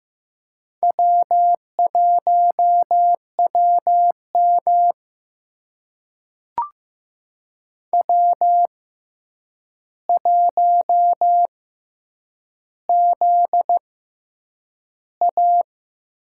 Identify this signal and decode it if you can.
Morse code, heard as a keyed tone, about 15 wpm, spaced as for 8 wpm, 700 Hz. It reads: W1WM W1ZA
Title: USA Call Signs - 4 Characters - 6x Character Spacing - 15wpm